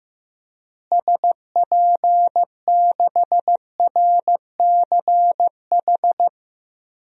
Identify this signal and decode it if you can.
Morse code, heard as a keyed tone, about 16 wpm, 700 Hz. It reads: SP6RCH
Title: DXCC-100 Call Signs - 6 Characters - Repeated 3x - 15wpm